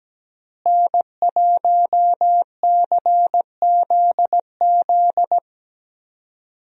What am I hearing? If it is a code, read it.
Morse code, heard as a keyed tone, about 17 wpm, 700 Hz. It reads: N1CZZ
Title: USA Call Signs - 5 Characters - Repeated 3x - 17wpm